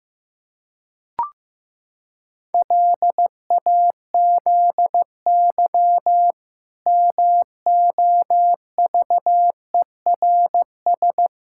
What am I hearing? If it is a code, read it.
Morse code, heard as a keyed tone, about 15 wpm, 700 Hz. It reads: LAZY MOVERS
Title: Pangram Challenge v2 - 15wpm